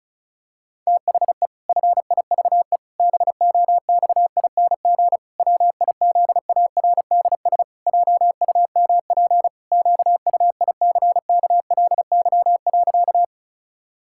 Morse code, heard as a keyed tone, about 36 wpm, 700 Hz. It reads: THE FIVE BOXING WIZARDS JUMP QUICKLY.
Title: Pangram Challenge v2 - 35wpm